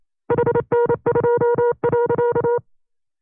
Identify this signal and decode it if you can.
Morse code, heard as a keyed tone, about 28 wpm, 480 Hz. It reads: HN2.
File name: FreeFiles / CW-practice / 25WPM-side2